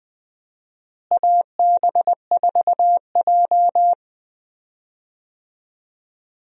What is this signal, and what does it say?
Morse code, heard as a keyed tone, about 20 wpm, 700 Hz. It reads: AB4J
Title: USA Call Signs - 4 Characters - Repeated 3x - 20wpm